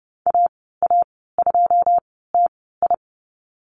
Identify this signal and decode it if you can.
Morse code, heard as a keyed tone, about 30 wpm, spaced as for 17 wpm, 700 Hz. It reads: AA2TI
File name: Lesson 14g - Number 2 - Call Signs 30wpm